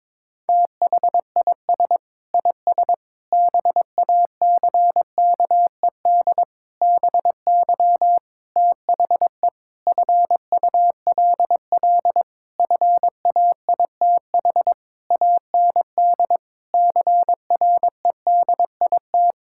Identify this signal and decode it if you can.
Morse code, heard as a keyed tone, about 22 wpm, 700 Hz. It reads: THIS IS BACKED BY THE FULL FAITH AND CREDIT
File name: Sentences with 9-Word Phrases - 22wpm